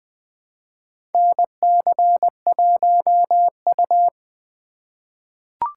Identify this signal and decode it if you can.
Morse code, heard as a keyed tone, about 20 wpm, 700 Hz. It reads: NC1U
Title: USA Call Signs - 4 Characters - 3x Character Spacing - 20wpm